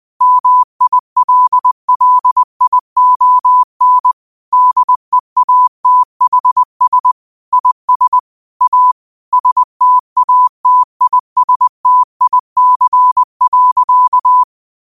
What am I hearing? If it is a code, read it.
Morse code, heard as a keyed tone, about 20 wpm, 1000 Hz. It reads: MILLION DEATHS IS A STATISTIC.